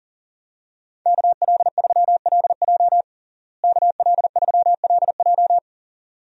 Morse code, heard as a keyed tone, about 37 wpm, 700 Hz. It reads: KL3LJ KL3LJ
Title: USA Call Signs - 5 Characters - Repeated 2x - 40wpm